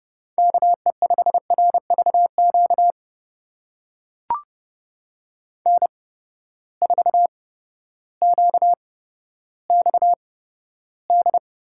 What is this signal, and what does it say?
Morse code, heard as a keyed tone, about 30 wpm, spaced as for 14 wpm, 700 Hz. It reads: KE5RVQ N4QXD
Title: USA Call Signs - 8x Character Spacing - 30wpm